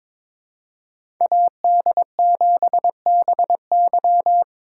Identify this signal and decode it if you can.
Morse code, heard as a keyed tone, about 22 wpm, 700 Hz. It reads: AD7BY
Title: USA Call Signs - 5 Characters - 4x Character Spacing - 22wpm